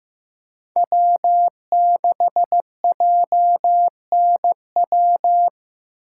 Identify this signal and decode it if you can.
Morse code, heard as a keyed tone, about 18 wpm, 700 Hz. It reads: W6JNW
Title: Speed Racing - USA 1x3 Call Signs 15wpm